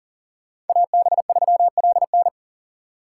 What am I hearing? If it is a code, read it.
Morse code, heard as a keyed tone, about 53 wpm, 700 Hz. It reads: AB3LN